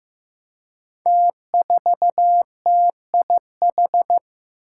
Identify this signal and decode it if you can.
Morse code, heard as a keyed tone, about 15 wpm, 700 Hz. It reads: T4TIH
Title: Five Challenging Letter-Number Combinations 15wpm